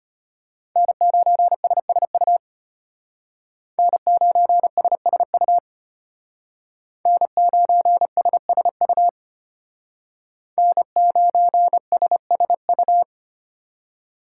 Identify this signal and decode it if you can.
Morse code, heard as a keyed tone, about 32 wpm, 700 Hz. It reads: N9SSU N9SSU N9SSU N9SSU